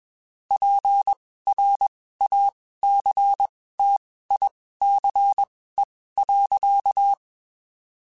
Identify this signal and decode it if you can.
Morse code, heard as a keyed tone, about 21 wpm, spaced as for 15 wpm, 800 Hz. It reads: PRACTICE.
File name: (spaced) / QSOs Spaced-mp3s / WD0FBT @20 sp